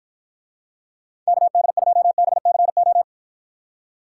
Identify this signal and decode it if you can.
Morse code, heard as a keyed tone, about 54 wpm, 700 Hz. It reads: KD2BCO